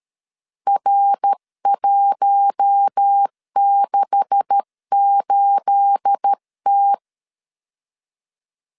Morse code, heard as a keyed tone, about 13 wpm, 800 Hz. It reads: R168T
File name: LEZIONI / 9 286 12wpm 2305